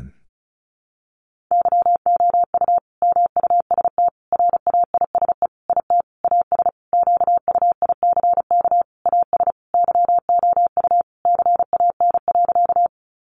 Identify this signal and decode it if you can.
Morse code, heard as a keyed tone, about 34 wpm, 700 Hz. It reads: YOU MUST RAISE IT AS QUICK AS YOU CAN.
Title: Sentences from Top 1400 Words 35wpm